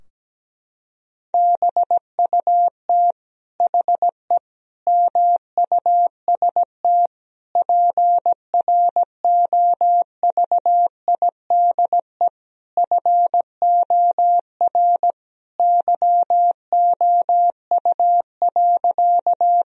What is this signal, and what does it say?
Morse code, heard as a keyed tone, about 17 wpm, 700 Hz. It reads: BUT HE MUST PROVIDE FOR YOU.
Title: Sentences from Top 1000 Words - Review Part 3 - 17wpm